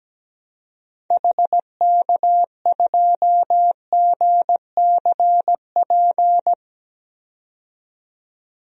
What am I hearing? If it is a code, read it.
Morse code, heard as a keyed tone, about 17 wpm, 700 Hz. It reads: HK2GCP
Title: DXCC-100 Call Signs - 6 Characters - 17wpm